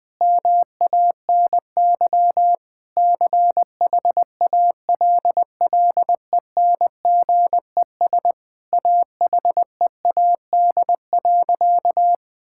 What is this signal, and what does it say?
Morse code, heard as a keyed tone, about 21 wpm, 700 Hz. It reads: MANY CHALLENGES AHEAD.
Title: Sentences with 5-Word Phrases - Spoken Beginning - 20wpm